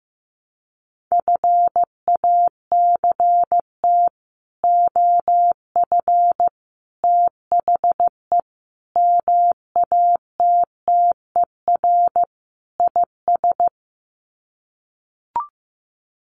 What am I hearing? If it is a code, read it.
Morse code, heard as a keyed tone, about 15 wpm, 700 Hz. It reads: FACT OF THE MATTER IS
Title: Sets of 5 Words 15wpm